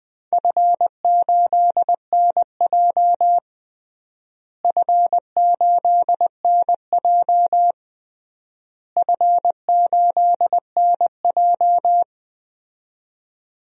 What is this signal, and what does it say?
Morse code, heard as a keyed tone, about 20 wpm, 700 Hz. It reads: F8NJ F8NJ F8NJ
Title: DXCC-100 Call Signs - 4 Characters - Repeated 3x - 20wpm